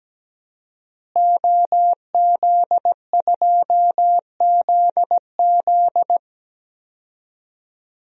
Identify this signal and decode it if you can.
Morse code, heard as a keyed tone, about 17 wpm, 700 Hz. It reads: OZ2ZZ
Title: DXCC-100 Call Signs - 5 Characters - Repeated 2x - 17wpm